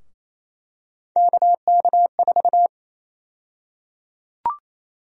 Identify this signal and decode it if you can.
Morse code, heard as a keyed tone, about 28 wpm, 700 Hz. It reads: KK4
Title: USA Call Sign Prefixes - 28wpm